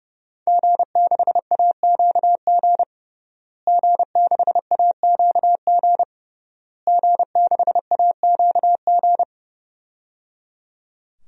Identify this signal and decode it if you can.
Morse code, heard as a keyed tone, about 28 wpm, 700 Hz. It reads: G6AQG G6AQG G6AQG
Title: DXCC-100 Call Signs - 5 Characters - Repeated 3x - 30wpm